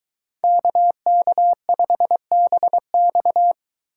Morse code, heard as a keyed tone, about 22 wpm, 700 Hz. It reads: KK5BX